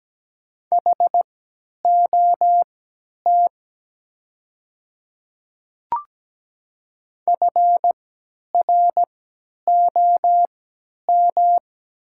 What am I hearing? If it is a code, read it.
Morse code, heard as a keyed tone, about 17 wpm, spaced as for 10 wpm, 690 Hz. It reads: HOT FROM
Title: 3x Character Spacing - Top 100 Words - 17wpm